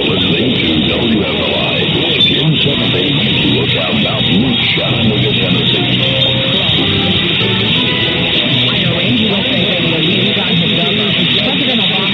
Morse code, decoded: none

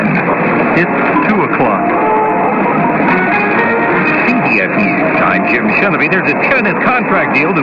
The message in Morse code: none

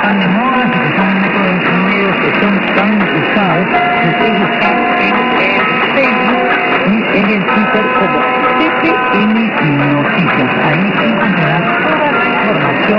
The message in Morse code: none

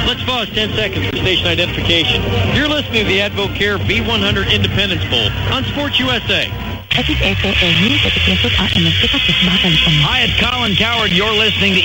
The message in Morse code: none

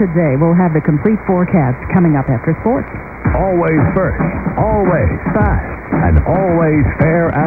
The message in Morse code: none

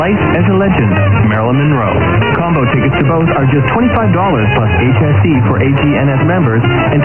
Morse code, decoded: none

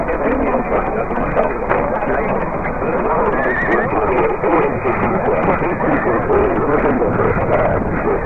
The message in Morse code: none

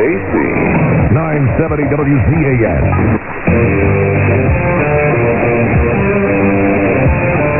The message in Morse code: none

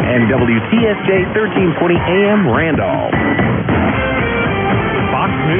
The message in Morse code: none

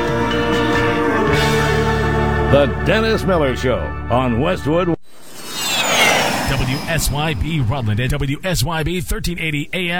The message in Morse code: none

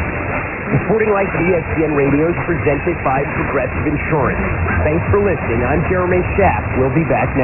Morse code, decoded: none

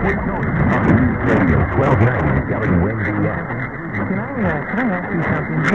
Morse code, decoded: none

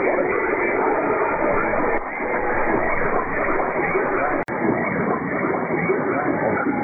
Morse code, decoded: none

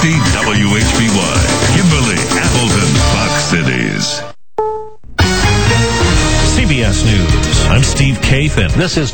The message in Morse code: none

